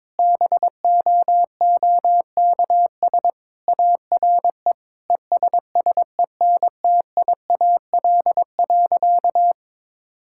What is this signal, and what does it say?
Morse code, heard as a keyed tone, about 22 wpm, 700 Hz. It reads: BOOKS ARE ESSENTIAL.
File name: practice / 22wpm / Sentences with 8-Word Phrases - 22wpm